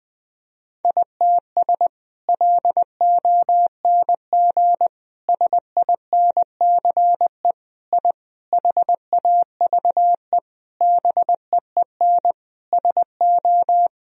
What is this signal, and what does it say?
Morse code, heard as a keyed tone, about 20 wpm, 700 Hz. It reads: ITS LONG SINCE I HAVE BEEN SO
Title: Sentences from Top 1000 Words - Review Part 1 - 20wpm